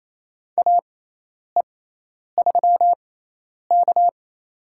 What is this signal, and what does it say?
Morse code, heard as a keyed tone, about 28 wpm, spaced as for 10 wpm, 700 Hz. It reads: AE3K
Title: USA Call Signs - 4 Characters - 6x Character Spacing - 28wpm